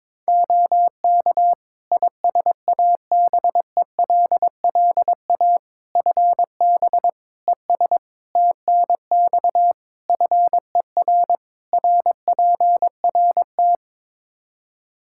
Morse code, heard as a keyed tone, about 22 wpm, 700 Hz. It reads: OK ISABELLA FB ES TNX FER RPRT
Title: Practice QSO - Exchange 2 - Info - 22wpm